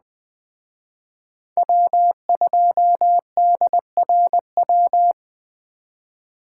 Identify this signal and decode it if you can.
Morse code, heard as a keyed tone, about 20 wpm, 700 Hz. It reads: W2DRW
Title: Speed Racing - USA 1x3 Call Signs 17wpm